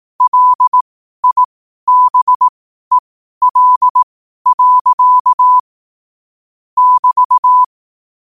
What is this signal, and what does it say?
Morse code, heard as a keyed tone, about 18 wpm, spaced as for 12 wpm, 1000 Hz. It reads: LIBEL. <BT>
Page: qotd-at-12-wpm